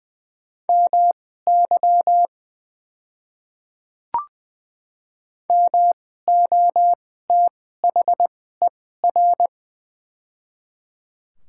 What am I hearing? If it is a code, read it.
Morse code, heard as a keyed tone, about 20 wpm, spaced as for 14 wpm, 700 Hz. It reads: MY MOTHER
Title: 2x Character Spacing - Top 200 Words - 20wpm